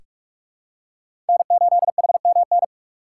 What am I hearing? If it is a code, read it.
Morse code, heard as a keyed tone, about 41 wpm, 700 Hz. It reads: N8HMN
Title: USA Call Signs - 5 Characters - 4x Character Spacing - 45wpm